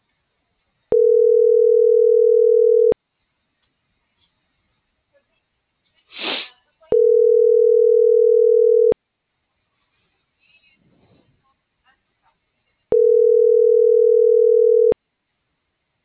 Ringback tone (North America), 440 + 480 Hz.